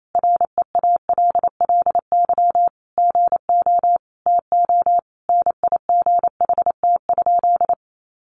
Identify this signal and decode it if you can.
Morse code, heard as a keyed tone, about 28 wpm, 700 Hz. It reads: REALLY GO TO NIGHT?
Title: Sentences from Top 600 Words 28wpm